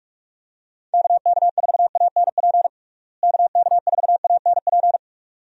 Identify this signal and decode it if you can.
Morse code, heard as a keyed tone, about 46 wpm, 700 Hz. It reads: KK4ANP KK4ANP
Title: USA Call Signs - 6 Characters - Repeated 2x - 45wpm